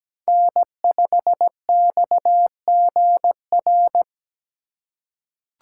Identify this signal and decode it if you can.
Morse code, heard as a keyed tone, about 17 wpm, 700 Hz. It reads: N5XGR
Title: USA Call Signs - 3x Character Spacing - 17wpm